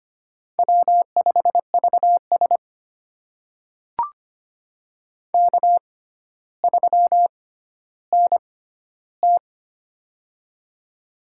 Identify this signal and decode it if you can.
Morse code, heard as a keyed tone, about 25 wpm, spaced as for 13 wpm, 700 Hz. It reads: W5VS K3NT
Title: USA Call Signs - 4 Characters - 6x Character Spacing - 25wpm